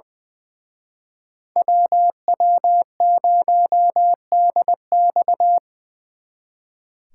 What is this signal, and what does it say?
Morse code, heard as a keyed tone, about 25 wpm, 700 Hz. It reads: WW0DX
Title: Speed Racing - USA 2x2 Call Signs 20wpm